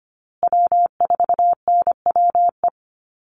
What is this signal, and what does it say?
Morse code, heard as a keyed tone, about 25 wpm, 700 Hz. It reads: W4NWE
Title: USA Call Signs Repeated 2x 25wpm